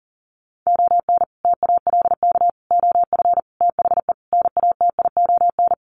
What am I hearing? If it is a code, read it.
Morse code, heard as a keyed tone, about 40 wpm, 700 Hz. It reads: ON TALK OF THE NATION
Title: Sets of 5 Words 40wpm